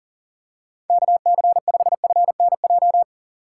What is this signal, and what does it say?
Morse code, heard as a keyed tone, about 37 wpm, 700 Hz. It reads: KC5FNJ
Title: USA Call Signs - 6 Characters - 2x Character Spacing - 40wpm